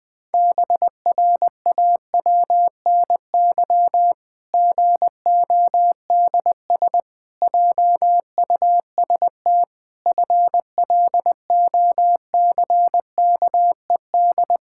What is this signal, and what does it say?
Morse code, heard as a keyed tone, about 20 wpm, 700 Hz. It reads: BRAWNY GODS JUST FLOCKED
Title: Pangram Challenge v2 - 20wpm